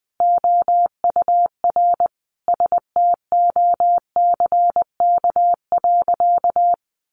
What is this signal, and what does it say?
Morse code, heard as a keyed tone, about 20 wpm, 700 Hz. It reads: OUR STOCK.
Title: Sentences from Top 1300 Words 20wpm